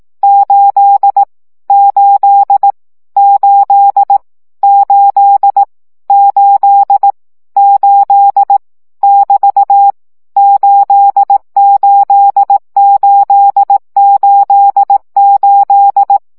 Morse code, heard as a keyed tone, about 18 wpm, 800 Hz. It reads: 8 8 8 8 8 8 <BT> 88888
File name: CW gyakorlas 8 15 baud